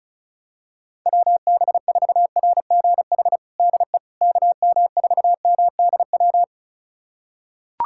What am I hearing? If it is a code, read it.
Morse code, heard as a keyed tone, about 34 wpm, 700 Hz. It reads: WB4RGH DE KM4MDW
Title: USA Callsign DE Callsign 35wpm